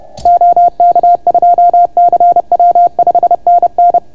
Morse code, heard as a keyed tone, about 30 wpm, 690 Hz. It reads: OK2CW 5NN